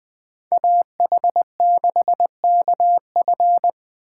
Morse code, heard as a keyed tone, about 21 wpm, 700 Hz. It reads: AH6KF